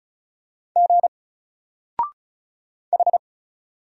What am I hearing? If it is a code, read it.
Morse code, heard as a keyed tone, about 34 wpm, 700 Hz. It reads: G H
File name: Single-Letter-Number-35wpm